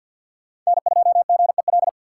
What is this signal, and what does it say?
Morse code, heard as a keyed tone, about 59 wpm, 700 Hz. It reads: N2GEL
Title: Speed Racing - USA 1x3 Call Signs 50wpm